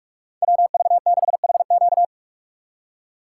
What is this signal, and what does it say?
Morse code, heard as a keyed tone, about 44 wpm, 700 Hz. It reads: WV6HQ
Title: USA Call Signs - 5 Characters - 4x Character Spacing - 45wpm